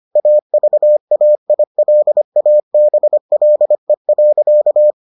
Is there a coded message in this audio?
Yes: Morse code from 0.2 to 4.9 s, heard as a keyed tone, about 25 wpm, 600 Hz. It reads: AVAILABLE.